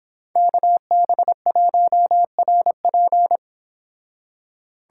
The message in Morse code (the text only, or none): KB1RP